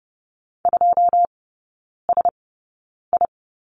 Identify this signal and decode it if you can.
Morse code, heard as a keyed tone, about 30 wpm, spaced as for 9 wpm, 700 Hz. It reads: 2SI
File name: 081 Lesson 14e - Number 2 - Three Characters 30wpm